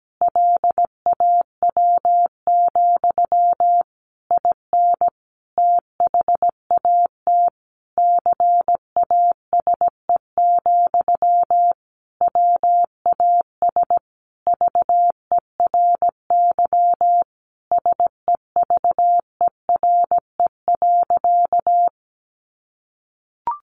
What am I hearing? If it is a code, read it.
Morse code, heard as a keyed tone, about 17 wpm, 700 Hz. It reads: LAW, IN THAT CASE, WAS VERY SEVERE.
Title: Sentences from Top 1400 Words 17wpm